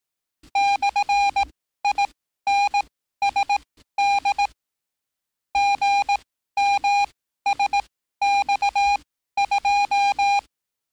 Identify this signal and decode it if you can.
Morse code, heard as a keyed tone, about 18 wpm, spaced as for 13 wpm, 810 Hz. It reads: /INSD GMSX2
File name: training / NAVPERS 11049to11060 / NAVPERS 11054 RA RB